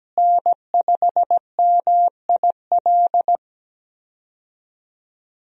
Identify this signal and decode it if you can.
Morse code, heard as a keyed tone, about 17 wpm, 700 Hz. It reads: N5MIL
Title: USA Call Signs - 5 Characters - Repeated 3x - 17wpm